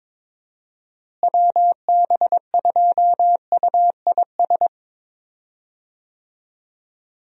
Morse code, heard as a keyed tone, about 22 wpm, 700 Hz. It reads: WB2UIS
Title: USA Call Signs - 6 Characters - Repeated 2x - 22wpm